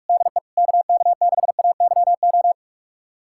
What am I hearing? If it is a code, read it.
Morse code, heard as a keyed tone, about 46 wpm, 700 Hz. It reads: DE KK6AYO
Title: USA Callsign DE Callsign 45wpm